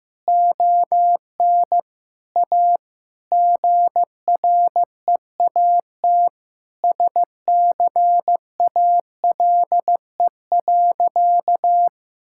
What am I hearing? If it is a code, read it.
Morse code, heard as a keyed tone, about 15 wpm, 700 Hz. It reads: ON A GREAT SCALE.
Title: Sentences from Top 1000 Words - Review Part 5 - 15wpm